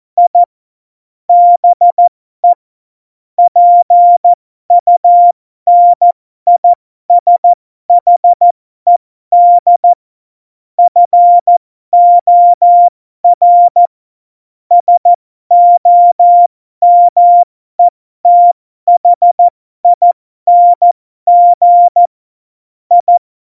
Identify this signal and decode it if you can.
Morse code, heard as a keyed tone, about 14 wpm, spaced as for 12 wpm, 700 Hz. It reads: I BE PUNISHED FOR SOMETHING I
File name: Morsekurs / Witze